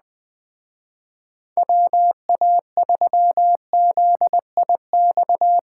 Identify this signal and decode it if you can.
Morse code, heard as a keyed tone, about 21 wpm, 700 Hz. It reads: WA3ZIX
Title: Speed Racing - USA Call Signs 17wpm